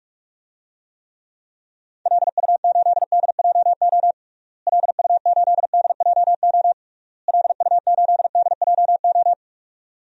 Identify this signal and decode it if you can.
Morse code, heard as a keyed tone, about 41 wpm, 700 Hz. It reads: LU8DJO LU8DJO LU8DJO
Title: DXCC-100 Call Signs - 6 Characters - Repeated 3x - 45wpm